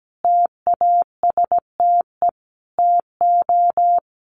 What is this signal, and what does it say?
Morse code, heard as a keyed tone, about 17 wpm, 700 Hz. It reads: TASTE TO
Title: Sentences from Top 1200 Words 17wpm